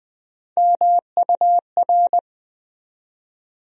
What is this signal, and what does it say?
Morse code, heard as a keyed tone, about 20 wpm, 700 Hz. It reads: MUR
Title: USA Call Sign Postfixes - 3 Letters 20wpm